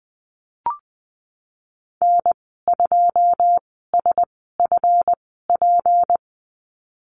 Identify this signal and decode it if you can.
Morse code, heard as a keyed tone, about 20 wpm, spaced as for 15 wpm, 700 Hz. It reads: N2SFP
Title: 186 Lesson 30g - Letter b - Call Signs 20wpm